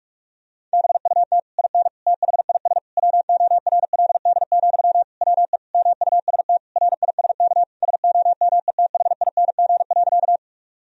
Morse code, heard as a keyed tone, about 46 wpm, 700 Hz. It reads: BUT IN THIS WORLD, WE MUST RISK SOMETHING.
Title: Sentences from Top 1000 Words - Review Part 4 - 45wpm